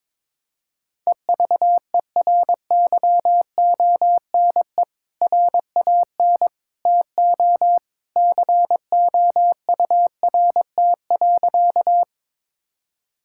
Morse code, heard as a keyed tone, about 21 wpm, 700 Hz. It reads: EVERYONE RAN TO COURT.
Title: Sentences from Top 1600 Words 22wpm